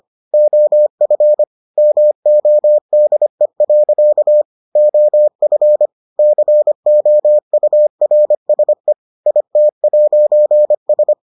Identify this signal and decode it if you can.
Morse code, heard as a keyed tone, about 25 wpm, 600 Hz. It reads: OF MODE. OF COURSE IT'S